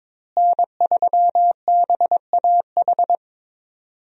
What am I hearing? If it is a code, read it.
Morse code, heard as a keyed tone, about 22 wpm, 700 Hz. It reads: N3BAH
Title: USA Call Signs - 5 Characters - Repeated 2x - 22wpm